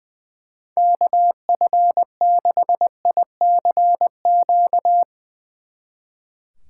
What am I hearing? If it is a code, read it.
Morse code, heard as a keyed tone, about 20 wpm, 700 Hz. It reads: KF6ICQ
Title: Speed Racing - USA Call Signs 20wpm